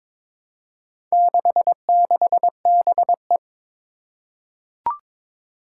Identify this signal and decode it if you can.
Morse code, heard as a keyed tone, about 23 wpm, 700 Hz. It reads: 66BE
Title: Four Challenging Letter-Number Combinations 22wpm